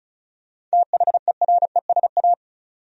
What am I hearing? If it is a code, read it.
Morse code, heard as a keyed tone, about 36 wpm, 700 Hz. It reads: THERESA
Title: USA Common Names - 35wpm